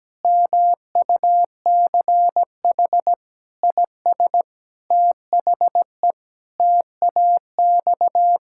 Morse code, heard as a keyed tone, about 17 wpm, 700 Hz. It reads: MUCH IS THE TAX
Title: Sentences from Top 1000 Words - Review Part 5 - 17wpm